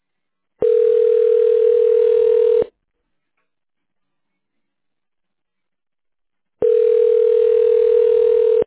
Ringback tone (North America), 440 + 480 Hz.
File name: 1040.3000.b387cb39-f6e9-41dc-8445